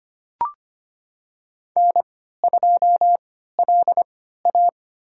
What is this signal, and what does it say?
Morse code, heard as a keyed tone, about 25 wpm, spaced as for 14 wpm, 700 Hz. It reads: N2LA
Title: USA Call Signs - 3x Character Spacing - 25wpm